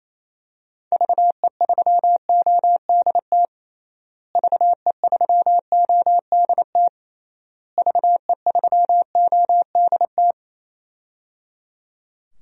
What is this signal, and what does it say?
Morse code, heard as a keyed tone, about 28 wpm, 700 Hz. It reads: VE3ODT VE3ODT VE3ODT
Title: DXCC-100 Call Signs - 6 Characters - Repeated 3x - 28wpm